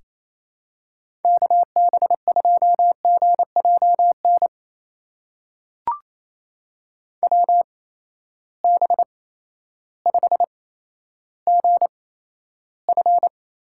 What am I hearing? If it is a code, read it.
Morse code, heard as a keyed tone, about 28 wpm, spaced as for 13 wpm, 700 Hz. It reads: KB2GJN WB5GF
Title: USA Call Signs - 6 Characters - 8x Character Spacing - 28wpm